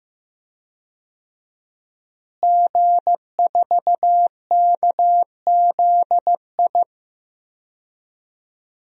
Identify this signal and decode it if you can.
Morse code, heard as a keyed tone, about 15 wpm, 700 Hz. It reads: G4KZI
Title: DXCC-100 Call Signs - 5 Characters - Repeated 2x - 15wpm